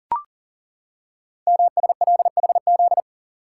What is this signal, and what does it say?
Morse code, heard as a keyed tone, about 43 wpm, 700 Hz. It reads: MSLHZ
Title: Code Groups - 5 Letter:Numbers - 40wpm